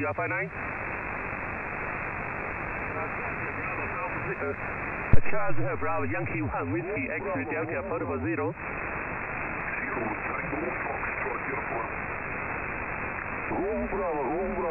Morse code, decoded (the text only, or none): none